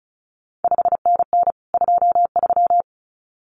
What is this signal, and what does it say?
Morse code, heard as a keyed tone, about 36 wpm, 700 Hz. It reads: 5NN 23
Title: CQ WW DX Contest Exchange Encore - 35wpm